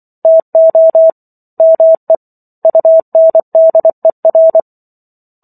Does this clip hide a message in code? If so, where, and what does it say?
Yes: Morse code from 0.2 to 4.6 s, heard as a keyed tone, about 24 wpm, 650 Hz. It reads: TO ME UNDER